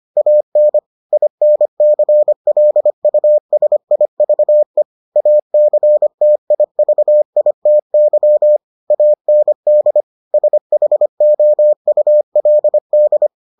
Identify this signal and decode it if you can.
Morse code, heard as a keyed tone, about 25 wpm, 600 Hz. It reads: AN INCLUSIVE ACTIVITY AND SHOULD